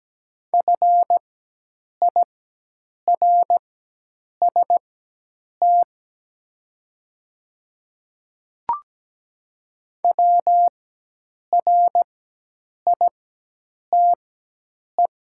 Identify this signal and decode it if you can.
Morse code, heard as a keyed tone, about 17 wpm, spaced as for 8 wpm, 700 Hz. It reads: FIRST WRITE
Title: 4x Character Spacing - Top 100 Words - 17wpm